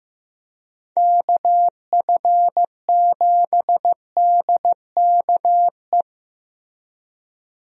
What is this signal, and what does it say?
Morse code, heard as a keyed tone, about 14 wpm, 700 Hz. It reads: KF7DKE
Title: USA Call Signs - 6 Characters - Repeated 3x - 15wpm